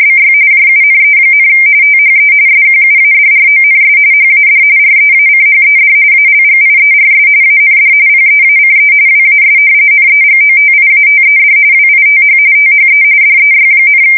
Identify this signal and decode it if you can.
RTTY, about 45.45 baud. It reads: BY JUNE, THE SQUADRONS HAD REACHED ALGIERS AND PEACE WAS NEGOTIATED WITH THE DEY,
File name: fsk170 / usnavy2